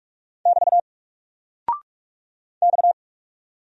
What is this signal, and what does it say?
Morse code, heard as a keyed tone, about 44 wpm, 700 Hz. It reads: <BT> X
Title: Single Character (Letter:Number:Punctuation:Prosign) 45wpm